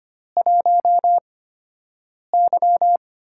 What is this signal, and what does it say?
Morse code, heard as a keyed tone, about 25 wpm, spaced as for 7 wpm, 700 Hz. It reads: 1Y
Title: USA Call Signs - 4 Characters - 8x Character Spacing - 25wpm